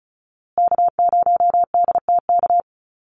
Morse code, heard as a keyed tone, about 34 wpm, 700 Hz. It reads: K0DTK
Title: USA Call Signs Repeated 3x 35wpm